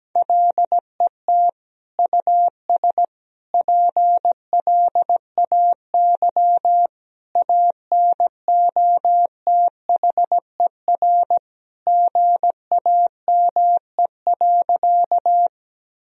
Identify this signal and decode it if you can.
Morse code, heard as a keyed tone, about 17 wpm, 700 Hz. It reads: LET US PLAY ANOTHER GAME.
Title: Sentences from Top 1000 Words - Review Part 3 - 17wpm